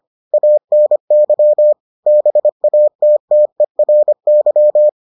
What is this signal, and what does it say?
Morse code, heard as a keyed tone, about 25 wpm, 600 Hz. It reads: ANY BATTERY